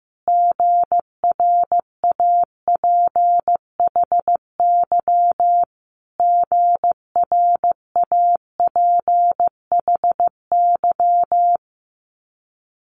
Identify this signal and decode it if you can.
Morse code, heard as a keyed tone, about 15 wpm, 700 Hz. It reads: GRAPHY GRAPHY
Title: Multi-letter Phonemes Repeated 3x 15wpm